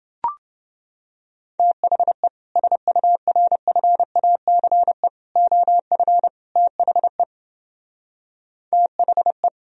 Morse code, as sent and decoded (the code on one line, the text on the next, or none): - .... / ... ..- .-. ..-. .- -.-. / --- ..-. / - .... / - .... .
THE SURFACE OF THE THE